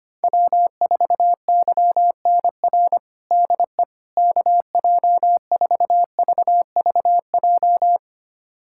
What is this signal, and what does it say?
Morse code, heard as a keyed tone, about 25 wpm, 700 Hz. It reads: W4YNR DE KJ4VVJ
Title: USA Callsign DE Callsign 25wpm